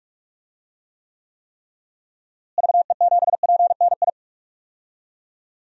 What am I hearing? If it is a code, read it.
Morse code, heard as a keyed tone, about 42 wpm, 700 Hz. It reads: VE7PNI
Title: DXCC-100 Call Signs - 6 Characters - Repeated 3x - 45wpm